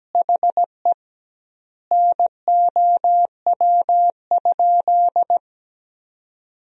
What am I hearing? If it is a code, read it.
Morse code, heard as a keyed tone, about 17 wpm, 700 Hz. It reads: HE NOW?
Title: 2x Word-Spacing - Sentences from Top 200 Words - 17wpm